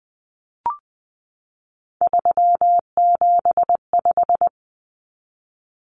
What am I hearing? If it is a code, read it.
Morse code, heard as a keyed tone, about 20 wpm, 700 Hz. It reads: 375
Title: Common Cut Numbers - 3-Digit 20wpm